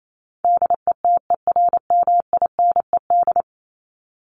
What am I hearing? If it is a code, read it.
Morse code, heard as a keyed tone, about 28 wpm, 700 Hz. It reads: DETERMINED
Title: ED Suffix - Top 100 Words with Base Words 28wpm